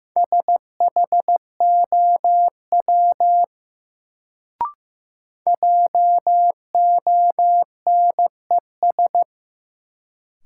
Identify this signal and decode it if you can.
Morse code, heard as a keyed tone, about 15 wpm, 700 Hz. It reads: SHOW JONES
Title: Sets of 5 Words - New Word Drill - 15wpm